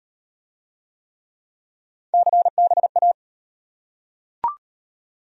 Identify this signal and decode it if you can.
Morse code, heard as a keyed tone, about 35 wpm, 700 Hz. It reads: CBA